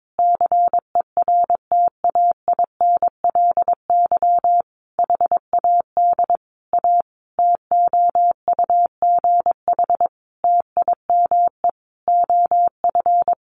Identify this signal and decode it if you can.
Morse code, heard as a keyed tone, about 22 wpm, 700 Hz. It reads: CERTAINLY HAD A TOUGH TIME OF
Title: Sentences from Top 1200 Words 22wpm